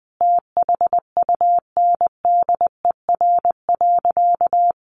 Morse code, heard as a keyed tone, about 20 wpm, 700 Hz. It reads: THUNDER.
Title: A Sound Of Thunder 20wpm